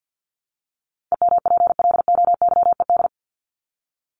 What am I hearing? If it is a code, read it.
Morse code, heard as a keyed tone, about 50 wpm, 700 Hz. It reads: EMPLOYED
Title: ED Suffix - Top 100 Words with Base Words 50wpm